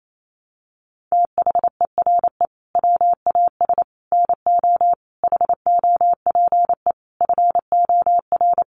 Morse code, heard as a keyed tone, about 28 wpm, 700 Hz. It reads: THERE WAS NO HOPE FOR
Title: Sentences from Top 1000 Words 28wpm